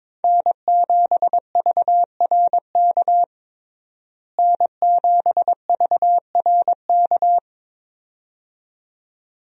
Morse code, heard as a keyed tone, about 22 wpm, 700 Hz. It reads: N7VRK N7VRK
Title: USA Call Signs - 5 Characters - Repeated 2x - 22wpm